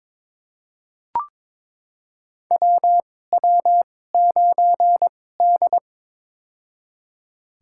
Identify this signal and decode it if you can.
Morse code, heard as a keyed tone, about 22 wpm, spaced as for 16 wpm, 700 Hz. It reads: WW9D
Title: USA Call Signs - 4 Characters - 2x Character Spacing - 22wpm